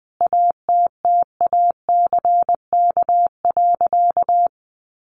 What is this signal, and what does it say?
Morse code, heard as a keyed tone, about 20 wpm, 700 Hz. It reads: ATTACK.
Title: Sentences from Top 1500 Words 20wpm